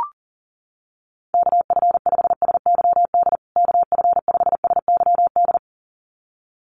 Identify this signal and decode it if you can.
Morse code, heard as a keyed tone, about 40 wpm, 700 Hz. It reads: KF5SYD KF5SYD
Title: USA Call Signs Repeated 2x 40wpm